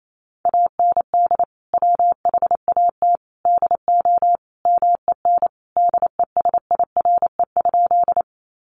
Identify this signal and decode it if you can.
Morse code, heard as a keyed tone, about 28 wpm, 700 Hz. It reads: AND WHAT DO MEN DESIRE?
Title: Sentences from Top 1100 Words 28wpm